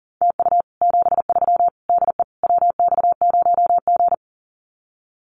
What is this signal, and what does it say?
Morse code, heard as a keyed tone, about 43 wpm, 700 Hz. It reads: TU 73 DE WX0G
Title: Common QSO Phrases Encore 40wpm